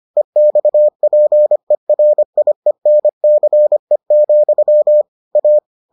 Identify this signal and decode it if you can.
Morse code, heard as a keyed tone, about 25 wpm, 600 Hz. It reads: EXPERIENCE, A